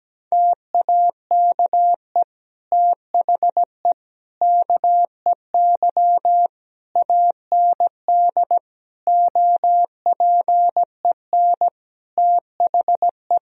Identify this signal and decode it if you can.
Morse code, heard as a keyed tone, about 17 wpm, 700 Hz. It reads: TAKE THE KEY AND OPEN THE
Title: Sentences from Top 1000 Words - Review Part 4 - 17wpm